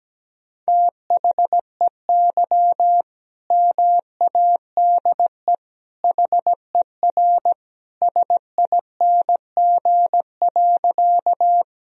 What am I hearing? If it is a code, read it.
Morse code, heard as a keyed tone, about 17 wpm, 700 Hz. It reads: THEY MADE HER SING.
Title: Sentences from Top 1000 Words - Review Part 2 - 17wpm